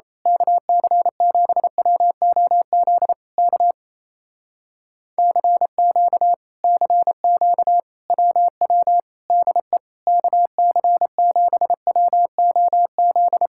Morse code, heard as a keyed tone, about 27 wpm, 700 Hz. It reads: KC7WOZ K CQ CQ WW DE KC7WOZ
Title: Speed-Racing Common QSO Phrases Encore - 28wpm